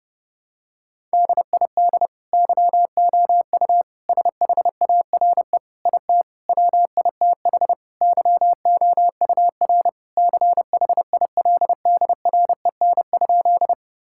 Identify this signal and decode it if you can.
Morse code, heard as a keyed tone, about 32 wpm, 700 Hz. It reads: DID YOU SHARE IT WITH YOUR CHILDREN?
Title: Sentences from Top 1000 Words - Review Part 6 - 30wpm